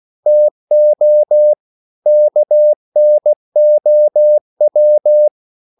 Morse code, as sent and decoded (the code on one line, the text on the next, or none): - --- / -.- -. --- .--
TO KNOW